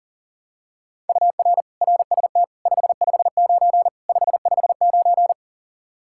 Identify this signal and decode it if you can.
Morse code, heard as a keyed tone, about 40 wpm, 700 Hz. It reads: UR RST 559 559
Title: Practice QSO - Exchange 1 - Beginning - 40wpm